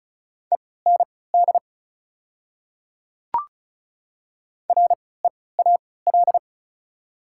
Morse code, heard as a keyed tone, about 36 wpm, spaced as for 20 wpm, 690 Hz. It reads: END REAL
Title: 3x Character Spacing - Top 300 Words - 35wpm